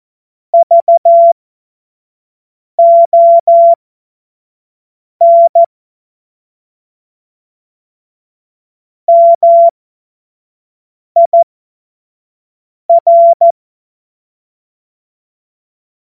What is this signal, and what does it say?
Morse code, heard as a keyed tone, about 14 wpm, spaced as for 5 wpm, 680 Hz. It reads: VON MIR
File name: Morsekurs / Witze / Witz 20. Dezember 2022 25 F70